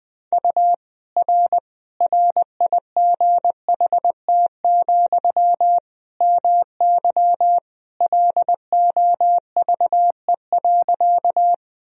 Morse code, heard as a keyed tone, about 20 wpm, 700 Hz. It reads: U R RIGHT, MY LOVE.
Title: Sentences from Top 400 Words and Abbreviations 20wpm